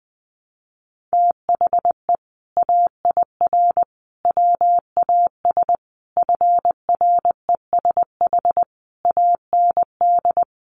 Morse code, heard as a keyed tone, about 20 wpm, 700 Hz. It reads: THE AIR WAS FRESH AND
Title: Sentences from Top 1200 Words 20wpm